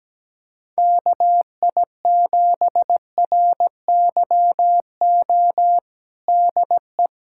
Morse code, heard as a keyed tone, about 17 wpm, 700 Hz. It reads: KI7RYO DE
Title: USA Callsign DE Callsign - 6 Characters - 17wpm